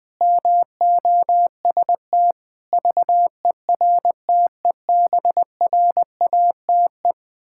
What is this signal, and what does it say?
Morse code, heard as a keyed tone, about 20 wpm, 700 Hz. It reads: MOST VERTEBRATE
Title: Pangram Challenge v2 - 20wpm